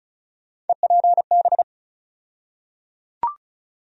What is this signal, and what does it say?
Morse code, heard as a keyed tone, about 34 wpm, 700 Hz. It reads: EPB
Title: USA Call Sign Postfixes - 3 Letters 35wpm